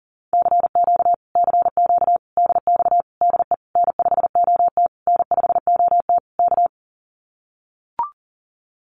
Morse code, heard as a keyed tone, about 40 wpm, 700 Hz. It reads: CQ CQ DX DE N5OT N5OT K
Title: Common QSO Phrases Encore 40wpm